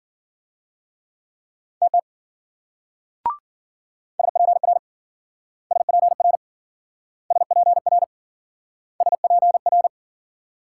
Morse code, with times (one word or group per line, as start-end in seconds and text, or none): 1.8-2.0 TT
4.2-4.8 SPR
5.7-6.4 SPR
7.3-8.0 SPR
9.0-9.9 SPR